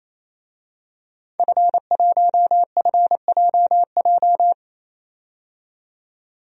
Morse code, heard as a keyed tone, about 28 wpm, 700 Hz. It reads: F1FJJ
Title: DXCC-100 Call Signs - 5 Characters - 28wpm